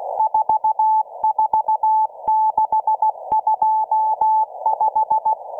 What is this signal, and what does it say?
Morse code, heard as a keyed tone, about 16 wpm, 870 Hz. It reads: 44625